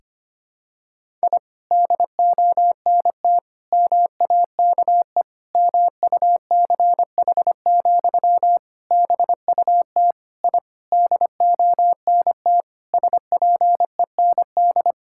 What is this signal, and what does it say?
Morse code, heard as a keyed tone, about 25 wpm, 700 Hz. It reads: I DONT MAKE MUCH, BUT I DONT SPEND
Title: Sentences from Top 1000 Words - Review Part 5 - 25wpm